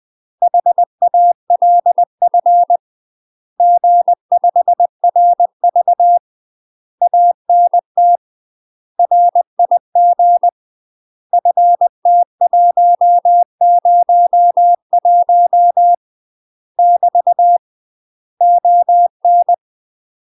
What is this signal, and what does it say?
Morse code, heard as a keyed tone, about 20 wpm, 700 Hz. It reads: HALF G5RV ANT RIG FT101 <BT> ON
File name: UK MASTER QSOs TRAINING / 20WPM / UK MASTER QSOs 0052